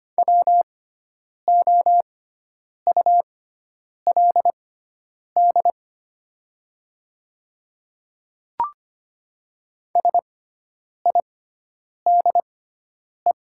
Morse code, heard as a keyed tone, about 25 wpm, spaced as for 9 wpm, 700 Hz. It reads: WOULD SIDE
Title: 6x Character Spacing - Top 100 Words - 25wpm